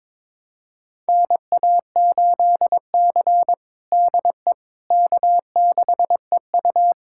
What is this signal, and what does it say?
Morse code, heard as a keyed tone, about 22 wpm, 700 Hz. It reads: NA8C DE K6EU
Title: USA Callsign DE Callsign - 4 Characters - 22wpm